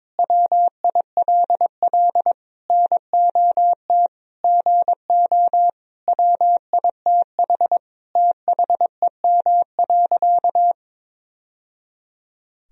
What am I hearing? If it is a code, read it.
Morse code, heard as a keyed tone, about 22 wpm, 700 Hz. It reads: WILL NOT GO WITH THEM.
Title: Sentences from Top 1000 Words - Review Part 4 - 22wpm